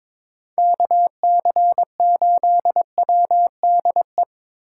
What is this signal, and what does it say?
Morse code, heard as a keyed tone, about 24 wpm, 700 Hz. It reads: KC8WDE
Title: Speed Racing - USA 2x3 Call Signs 22wpm